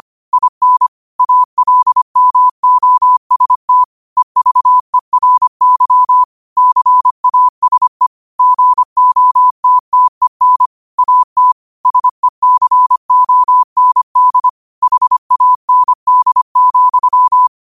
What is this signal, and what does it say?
Morse code, heard as a keyed tone, about 25 wpm, 1000 Hz. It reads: IN ALMOST EVERY CASE GOTTEN AT SECOND HAND,